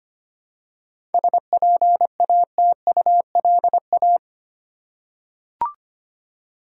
Morse code, heard as a keyed tone, about 25 wpm, 700 Hz. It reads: SPATULA
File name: practice / 25wpm / Easy Words with Hints - 25wpm